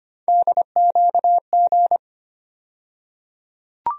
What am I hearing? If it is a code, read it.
Morse code, heard as a keyed tone, about 25 wpm, 700 Hz. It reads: DQG
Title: USA Call Sign Postfixes - 3 Letters 25wpm